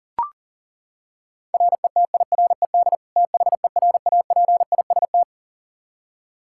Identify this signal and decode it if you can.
Morse code, heard as a keyed tone, about 40 wpm, 700 Hz. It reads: RETIRED THERAPIST
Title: Practice QSO - Exchange 3 - More Info - 40wpm